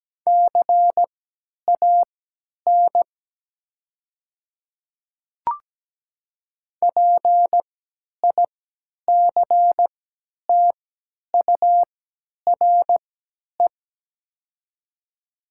Morse code, heard as a keyed tone, about 17 wpm, spaced as for 10 wpm, 700 Hz. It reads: CAN PICTURE
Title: 3x Character Spacing - Top 300 Words - 17wpm